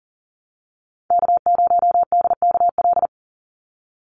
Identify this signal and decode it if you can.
Morse code, heard as a keyed tone, about 40 wpm, 690 Hz. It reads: K0DKL
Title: USA Call Signs 40wpm